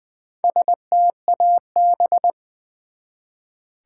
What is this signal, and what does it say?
Morse code, heard as a keyed tone, about 20 wpm, 700 Hz. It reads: STAB
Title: Build A Word - 1 Letter At A Time - 3 to 5-Letter Words - 20wpm